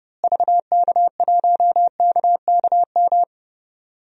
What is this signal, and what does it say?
Morse code, heard as a keyed tone, about 30 wpm, 700 Hz. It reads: VK1KKM
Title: DXCC-100 Call Signs - 6 Characters - Repeated 3x - 30wpm